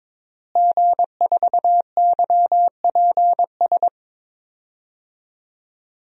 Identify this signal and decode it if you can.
Morse code, heard as a keyed tone, about 21 wpm, 700 Hz. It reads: G4YPS
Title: DXCC-100 Call Signs - 5 Characters - Repeated 2x - 22wpm